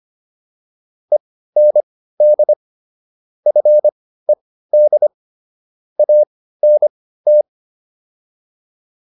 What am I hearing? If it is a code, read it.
Morse code, heard as a keyed tone, about 25 wpm, spaced as for 15 wpm, 600 Hz. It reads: END FED ANT